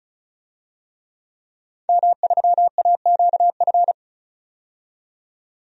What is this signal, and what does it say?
Morse code, heard as a keyed tone, about 34 wpm, 700 Hz. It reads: M3AQF
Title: DXCC-100 Call Signs - 5 Characters - 35wpm